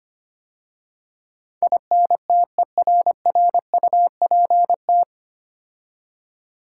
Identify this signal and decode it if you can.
Morse code, heard as a keyed tone, about 25 wpm, 700 Hz. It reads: INTERRUPT
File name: practice / 25wpm / 9 Letter Words - First 3 Letters Given - 25wpm